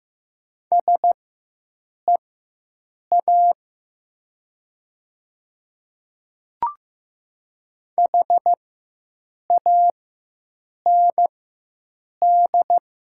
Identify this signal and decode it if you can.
Morse code, heard as a keyed tone, about 15 wpm, spaced as for 7 wpm, 700 Hz. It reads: SEA HAND
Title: 4x Character Spacing - Top 300 Words - 15wpm